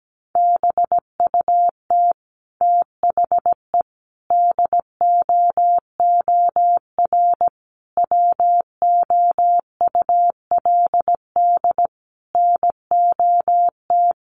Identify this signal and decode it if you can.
Morse code, heard as a keyed tone, about 17 wpm, 700 Hz. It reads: BUT THE DOOR WOULD NOT
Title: Sentences from Top 600 Words 17wpm